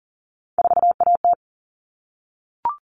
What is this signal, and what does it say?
Morse code, heard as a keyed tone, about 40 wpm, 700 Hz. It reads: 4AT
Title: All Cut Numbers - 3-Digit 40wpm